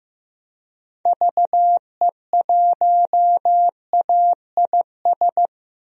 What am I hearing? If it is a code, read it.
Morse code, heard as a keyed tone, about 15 wpm, 700 Hz. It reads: VE1AIS
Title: DXCC-100 Call Signs - 6 Characters - Repeated 2x - 15wpm